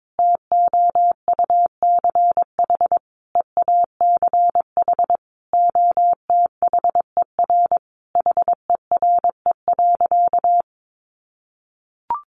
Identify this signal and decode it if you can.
Morse code, heard as a keyed tone, about 22 wpm, 700 Hz. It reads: TOUCH EACH OTHER HERE.
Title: Sentences from Top 1200 Words 22wpm